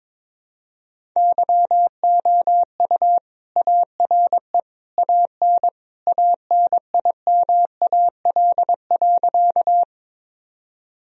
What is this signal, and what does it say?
Morse code, heard as a keyed tone, about 22 wpm, 700 Hz. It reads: YOU ARE AN ANIMAL.
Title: Sentences from Top 1000 Words - Review Part 1 - 22wpm